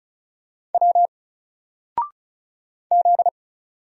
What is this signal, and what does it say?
Morse code, heard as a keyed tone, about 34 wpm, 700 Hz. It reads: W Z
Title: Single Letter-Number 35wpm